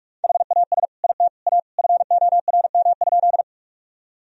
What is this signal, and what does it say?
Morse code, heard as a keyed tone, about 47 wpm, 700 Hz. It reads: HAS IT A FORM?
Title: Sentences from Top 1000 Words - Review Part 1 - 45wpm